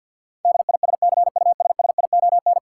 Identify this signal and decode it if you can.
Morse code, heard as a keyed tone, about 50 wpm, 700 Hz. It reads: DISCUSSION
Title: Sentences from Top 1100 Words - New Word Drill - 50wpm